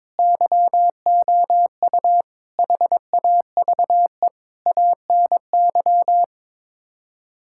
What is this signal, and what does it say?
Morse code, heard as a keyed tone, about 22 wpm, 700 Hz. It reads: YOU HAVE ANY
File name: practice / 22wpm / Sets of 7 Words - 22wpm